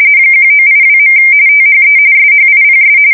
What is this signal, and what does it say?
RTTY, about 45.45 baud. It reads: 1982. THE SHIPS
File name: fsk170 / USS-New-Jersey